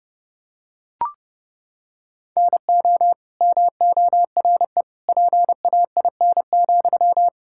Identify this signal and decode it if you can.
Morse code, heard as a keyed tone, about 32 wpm, 700 Hz. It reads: NO MORE PAIN,
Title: Sentences from Top 1600 Words 30wpm